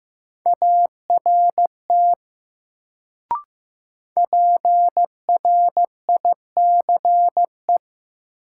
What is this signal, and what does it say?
Morse code, heard as a keyed tone, about 15 wpm, 700 Hz. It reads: ART PRICE
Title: Sentences from Top 600 Words - New Word Drill - 15wpm